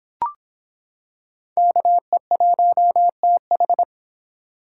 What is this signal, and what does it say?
Morse code, heard as a keyed tone, about 25 wpm, 700 Hz. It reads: KE1TH